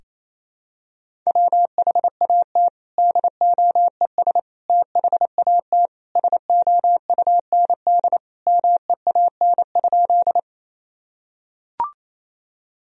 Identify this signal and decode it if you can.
Morse code, heard as a keyed tone, about 28 wpm, 700 Hz. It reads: WHAT DOES THAT SOUND MEAN?
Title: Sentences from Top 1000 Words - Review Part 1 - 28wpm